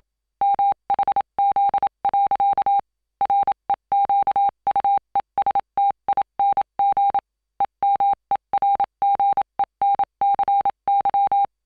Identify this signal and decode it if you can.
Morse code, heard as a keyed tone, about 27 wpm, 800 Hz. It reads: MHZ. REQUESTING EMERGENCY